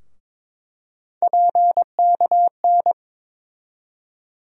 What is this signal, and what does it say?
Morse code, heard as a keyed tone, about 22 wpm, 700 Hz. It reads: PKN
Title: USA Call Sign Postfixes - 3 Letters 22wpm